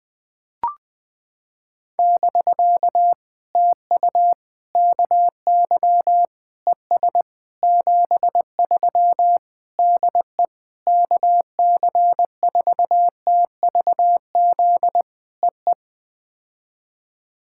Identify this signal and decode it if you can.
Morse code, heard as a keyed tone, about 20 wpm, 700 Hz. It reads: <BK> TU KY ES 73 DE KC4TVZ EE
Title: POTA QSO Practice - 20wpm